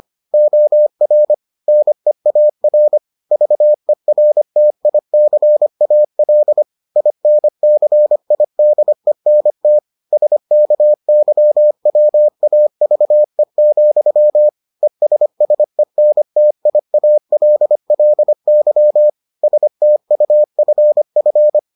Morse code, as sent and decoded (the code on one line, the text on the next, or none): --- .-. / -. .- .-. / ...- . .-. - .. -.-. .- .-.. / .. -. -.-. .. -.. -. - / ... -.- -.-- .-- .- ...- . --..-- / . ... ... -. - .. .- .-.. .-.. -.-- / ... - ..- ..-. ..-.
OR NEAR VERTICAL INCIDENT SKYWAVE, ESSENTIALLY STUFF